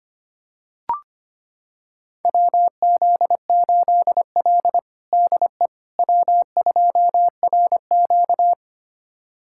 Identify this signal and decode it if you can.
Morse code, heard as a keyed tone, about 25 wpm, 700 Hz. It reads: WZ8L DE W2RQ
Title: USA Callsign DE Callsign - 4 Characters - 25wpm